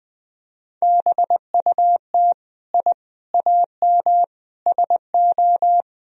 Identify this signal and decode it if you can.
Morse code, heard as a keyed tone, about 19 wpm, 700 Hz. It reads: BUT I AM SO
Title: Sentences from Top 1000 Words - Review Part 2 - 20wpm